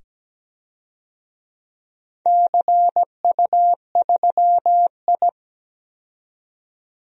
Morse code, heard as a keyed tone, about 17 wpm, 700 Hz. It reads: CU3I
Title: DXCC-100 Call Signs - 4 Characters - 17wpm